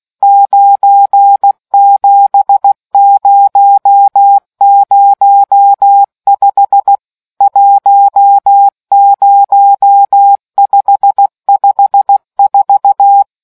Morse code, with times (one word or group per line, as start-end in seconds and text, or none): 0.2-7.0 97005
7.4-13.2 10554